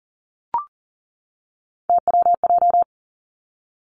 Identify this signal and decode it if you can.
Morse code, heard as a keyed tone, about 38 wpm, 700 Hz. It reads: TWJ
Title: Three Challenging Letter-Number Combinations 40wpm